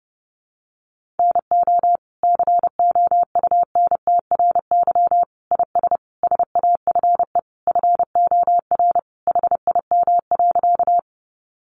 Morse code, heard as a keyed tone, about 30 wpm, 700 Hz. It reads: NO COUNTRY IS SAFE FOR HIM.
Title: Sentences from Top 900 Words 30wpm